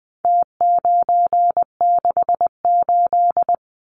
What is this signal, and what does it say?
Morse code, heard as a keyed tone, about 20 wpm, 700 Hz. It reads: T968
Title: Numbers - Contest Style - 4-Digit 20wpm